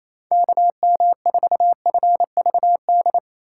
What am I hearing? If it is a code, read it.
Morse code, heard as a keyed tone, about 27 wpm, 700 Hz. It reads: KM4FVD
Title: USA Call Signs - 6x Character Spacing - 28wpm